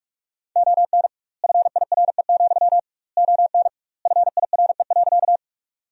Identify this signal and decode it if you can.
Morse code, heard as a keyed tone, about 48 wpm, 700 Hz. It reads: ON FIRE, ON FIRE.
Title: 2x Word-Spacing - Sentences from Top 400 Words - 45wpm